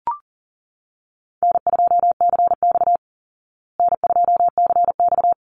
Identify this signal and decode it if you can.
Morse code, heard as a keyed tone, about 40 wpm, 700 Hz. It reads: N2CX N2CX
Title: Easy Call Signs Repeated 2x - 40wpm